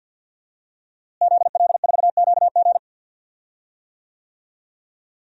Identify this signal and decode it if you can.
Morse code, heard as a keyed tone, about 53 wpm, 700 Hz. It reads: ZL4QG
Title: DXCC-100 Call Signs - 5 Characters - Repeated 2x - 50wpm